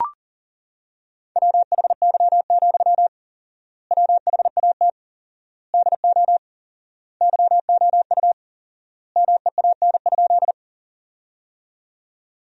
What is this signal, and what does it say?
Morse code, heard as a keyed tone, about 40 wpm, 700 Hz. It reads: WHY, WHAT DO YOU MEAN?
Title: 4x Word-Spacing - Sentences from Top 200 Words - 40wpm